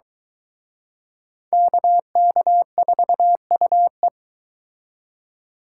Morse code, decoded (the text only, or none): KK4UE